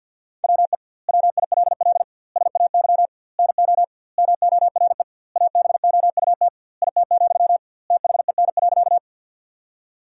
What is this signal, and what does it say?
Morse code, heard as a keyed tone, about 50 wpm, 700 Hz. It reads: WE WILL SAY NO MORE ABOUT IT, THEN.
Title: 2x Word-Spacing - Sentences from Top 200 Words - 50wpm